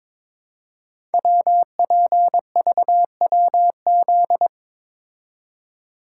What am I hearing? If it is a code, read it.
Morse code, heard as a keyed tone, about 22 wpm, 700 Hz. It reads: WPVWZ
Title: Code Groups - 5 Letters - 22wpm